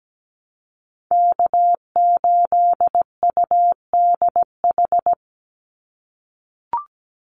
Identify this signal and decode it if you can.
Morse code, heard as a keyed tone, about 17 wpm, 700 Hz. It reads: K8UDH
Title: Call Signs - 17wpm